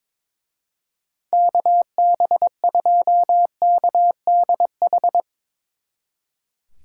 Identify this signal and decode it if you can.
Morse code, heard as a keyed tone, about 22 wpm, 700 Hz. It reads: KB2KDH
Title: Speed Racing - USA 2x3 Call Signs 22wpm